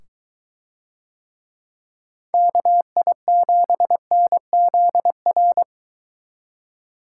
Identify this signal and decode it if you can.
Morse code, heard as a keyed tone, about 23 wpm, 700 Hz. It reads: KI7NZR